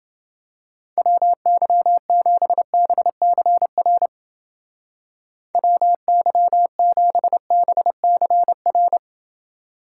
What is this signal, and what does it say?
Morse code, heard as a keyed tone, about 28 wpm, 700 Hz. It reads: WY7BCR WY7BCR